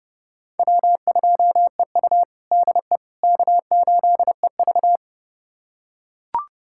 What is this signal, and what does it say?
Morse code, heard as a keyed tone, about 32 wpm, 700 Hz. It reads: W2EU DE K8EV
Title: USA Callsign DE Callsign - 4 Characters - 30wpm